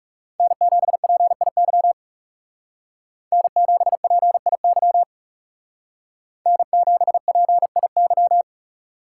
Morse code, heard as a keyed tone, about 38 wpm, 700 Hz. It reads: N7PIY N7PIY N7PIY